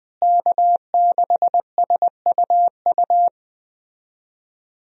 Morse code, heard as a keyed tone, about 19 wpm, 700 Hz. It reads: K6SUU
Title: USA Call Signs - 5 Characters - Repeated 3x - 20wpm